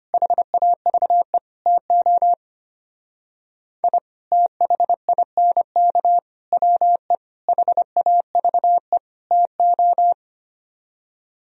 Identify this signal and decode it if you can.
Morse code, heard as a keyed tone, about 32 wpm, 700 Hz. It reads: HAVE TO I THINK WE HAVE TO